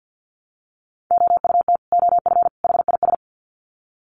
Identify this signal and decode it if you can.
Morse code, heard as a keyed tone, about 50 wpm, 700 Hz. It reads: OUT OF HIS
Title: Sets of 3 Words 50wpm